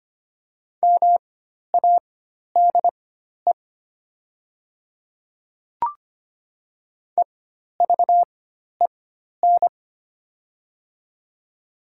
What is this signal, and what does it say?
Morse code, heard as a keyed tone, about 25 wpm, spaced as for 12 wpm, 700 Hz. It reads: MADE EVEN
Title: 4x Character Spacing - Top 300 Words - 25wpm